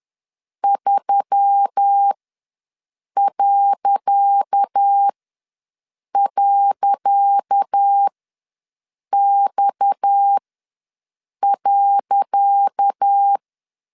Morse code, heard as a keyed tone, about 10 wpm, spaced as for 6 wpm, 800 Hz. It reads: 3..X.
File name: LEZIONI / 10 SIGNS / 10C DOT ws 10wpm 4213